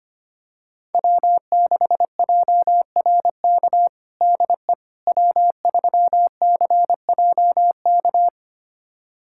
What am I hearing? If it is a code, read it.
Morse code, heard as a keyed tone, about 25 wpm, 700 Hz. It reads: W6JRK DE W3CJK
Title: USA Callsign DE Callsign - 5 Characters - 25wpm